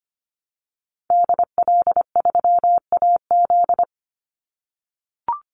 Morse code, heard as a keyed tone, about 25 wpm, 700 Hz. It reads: DL3AZ
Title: Call Signs - 25wpm